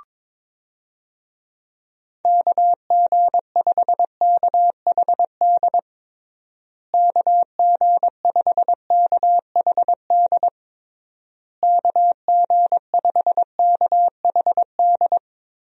Morse code, heard as a keyed tone, about 21 wpm, 700 Hz. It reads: KG5KHD KG5KHD KG5KHD
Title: USA Call Signs - 6 Characters - Repeated 3x - 22wpm